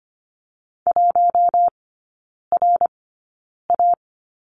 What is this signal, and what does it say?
Morse code, heard as a keyed tone, about 25 wpm, spaced as for 9 wpm, 700 Hz. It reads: 1RA
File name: course / 25:10wpm / 093 Lesson 16e - Letter c - Three Characters 25wpm